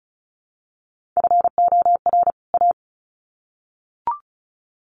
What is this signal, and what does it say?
Morse code, heard as a keyed tone, about 34 wpm, 700 Hz. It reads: FOR A
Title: Sets of 2 Words 35wpm